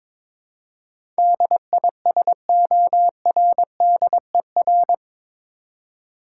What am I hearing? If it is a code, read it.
Morse code, heard as a keyed tone, about 22 wpm, 700 Hz. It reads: DISORDER
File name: practice / 22wpm / Eight Letter Words - 22wpm